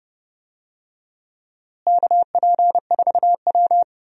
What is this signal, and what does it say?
Morse code, heard as a keyed tone, about 31 wpm, 700 Hz. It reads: KP4W
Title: DXCC-100 Call Signs - 4 Characters - Repeated 3x - 30wpm